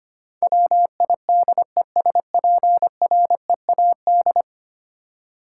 Morse code, heard as a keyed tone, about 25 wpm, 700 Hz. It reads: WIDESPREAD
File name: Ten Letter Words - 25wpm